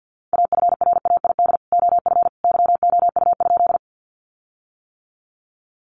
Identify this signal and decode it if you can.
Morse code, heard as a keyed tone, about 50 wpm, 700 Hz. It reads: AFRAID OF YOU?
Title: Sentences from Top 1500 Words 50wpm